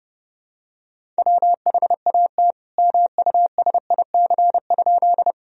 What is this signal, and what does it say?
Morse code, heard as a keyed tone, about 30 wpm, 700 Hz. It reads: WHAT MUSIC?
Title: Sentences from Top 1000 Words - Review Part 1 - 30wpm